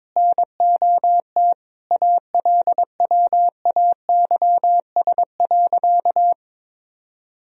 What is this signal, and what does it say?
Morse code, heard as a keyed tone, about 22 wpm, 700 Hz. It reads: NOT ALWAYS.
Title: Sentences from Top 1000 Words - Review Part 4 - 22wpm